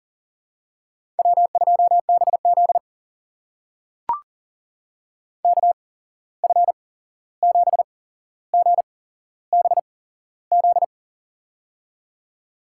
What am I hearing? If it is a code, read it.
Morse code, heard as a keyed tone, about 40 wpm, spaced as for 15 wpm, 700 Hz. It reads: W2BZ KF7GBZ
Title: USA Call Signs - 8x Character Spacing - 40wpm